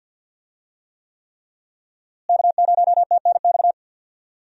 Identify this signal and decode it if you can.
Morse code, heard as a keyed tone, about 51 wpm, 700 Hz. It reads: K0TNX
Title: USA Call Signs - 5 Characters - Repeated 3x - 50wpm